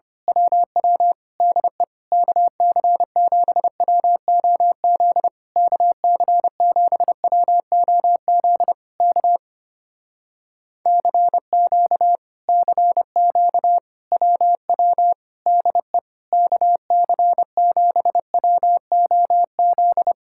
Morse code, heard as a keyed tone, about 24 wpm, 700 Hz. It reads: WW DE KC7WOZ KC7WOZ K CQ CQ WW DE KC7WOZ
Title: Speed-Racing Common QSO Phrases - 25wpm